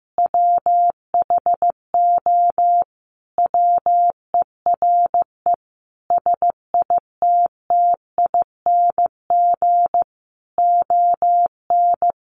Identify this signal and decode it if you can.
Morse code, heard as a keyed tone, about 14 wpm, 700 Hz. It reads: WHO WERE SITTING ON
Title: A Sound Of Thunder 15wpm